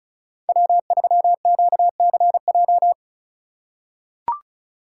Morse code, heard as a keyed tone, about 34 wpm, 700 Hz. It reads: W3QCJ
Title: USA Call Signs - 5 Characters - 4x Character Spacing - 35wpm